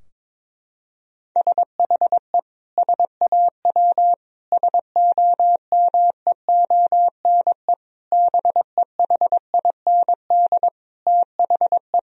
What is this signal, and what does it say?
Morse code, heard as a keyed tone, about 22 wpm, 700 Hz. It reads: SHE SAW SOMEONE BEHIND THE
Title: Sentences from Top 1000 Words - Review Part 4 - 22wpm